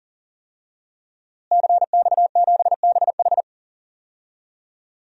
Morse code, heard as a keyed tone, about 40 wpm, 700 Hz. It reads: CX7BH
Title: DXCC-100 Call Signs - 5 Characters - Repeated 2x - 40wpm